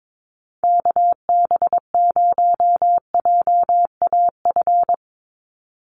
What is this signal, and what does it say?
Morse code, heard as a keyed tone, about 22 wpm, 700 Hz. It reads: KB0JAF
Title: USA Call Signs 22wpm